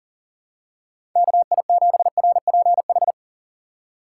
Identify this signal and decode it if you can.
Morse code, heard as a keyed tone, about 40 wpm, 700 Hz. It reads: KI7RPH
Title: USA Call Signs - 6 Characters - 2x Character Spacing - 40wpm